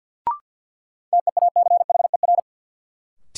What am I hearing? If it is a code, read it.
Morse code, heard as a keyed tone, about 52 wpm, 700 Hz. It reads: TEACHER
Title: Sentences from Top 900 Words - New Word Drill - 50wpm